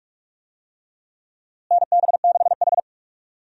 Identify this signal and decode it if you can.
Morse code, heard as a keyed tone, about 46 wpm, 700 Hz. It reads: NB6H
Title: USA Call Signs - 4 Characters - Repeated 3x - 45wpm